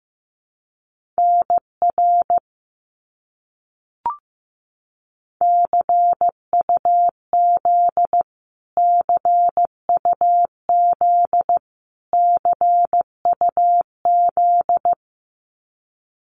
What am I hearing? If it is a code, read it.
Morse code, heard as a keyed tone, about 15 wpm, 700 Hz. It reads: NR CUZ CUZ CUZ
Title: Abbreviations Repeated 3x - 15wpm